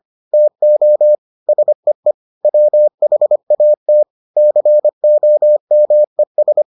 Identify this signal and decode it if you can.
Morse code, heard as a keyed tone, about 25 wpm, 600 Hz. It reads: TO SEE WHAT COMES